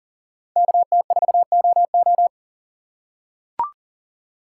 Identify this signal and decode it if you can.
Morse code, heard as a keyed tone, about 40 wpm, 700 Hz. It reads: KT4OO
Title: USA Call Signs - 6x Character Spacing - 40wpm